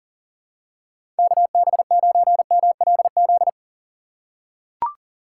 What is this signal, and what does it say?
Morse code, heard as a keyed tone, about 40 wpm, 700 Hz. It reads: KB9MLZ
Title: USA Call Signs - 6 Characters - 3x Character Spacing - 40wpm